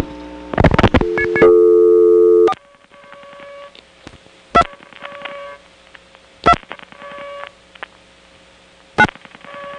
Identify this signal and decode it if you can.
DTMF, heard as telephone touch tones, tones of about 70 ms. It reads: 4 2 2 9
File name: phonetrips / SmyrnaGA5HQ